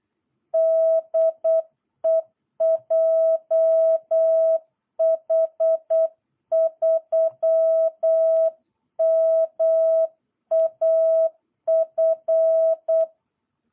Morse code, heard as a keyed tone, about 8 wpm, 660 Hz. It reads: DE JH3MAF
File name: jh8cbh / 2022cw